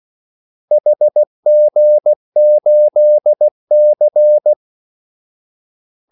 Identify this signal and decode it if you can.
Morse code, heard as a keyed tone, about 16 wpm, 600 Hz. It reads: HG8C
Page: Kaldesignaler 16 wpm